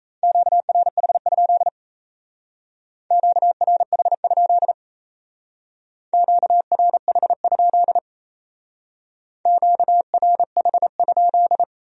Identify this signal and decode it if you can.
Morse code, heard as a keyed tone, about 37 wpm, 700 Hz. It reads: QRH? QRH? QRH? QRH?